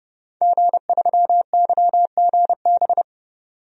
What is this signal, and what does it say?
Morse code, heard as a keyed tone, about 31 wpm, 700 Hz. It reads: G3YGB
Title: DXCC-100 Call Signs - 5 Characters - 30wpm